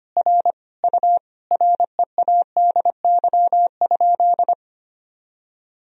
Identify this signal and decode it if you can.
Morse code, heard as a keyed tone, about 25 wpm, 700 Hz. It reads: R U READY?
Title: Sentences from Top 400 Words and Abbreviations 25wpm